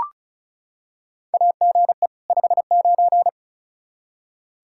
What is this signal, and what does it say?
Morse code, heard as a keyed tone, about 36 wpm, 700 Hz. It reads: AGE 59
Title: Practice QSO - Exchange 3 - More Info - 35wpm